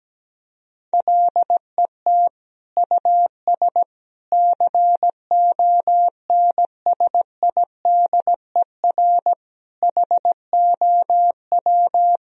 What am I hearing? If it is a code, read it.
Morse code, heard as a keyed tone, about 17 wpm, 700 Hz. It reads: LET US CONSIDER HOW
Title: Sentences from Top 1000 Words - Review Part 4 - 17wpm